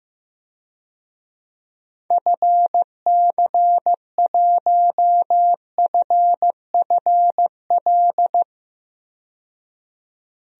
Morse code, heard as a keyed tone, about 15 wpm, 700 Hz. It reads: FC1FFL
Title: DXCC-100 Call Signs - 6 Characters - 15wpm